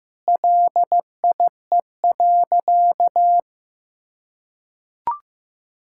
Morse code, heard as a keyed tone, about 15 wpm, 700 Hz. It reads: LIE.
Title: Sentences from Top 1600 Words 15wpm